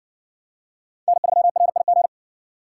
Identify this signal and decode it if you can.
Morse code, heard as a keyed tone, about 54 wpm, 700 Hz. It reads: N3RIG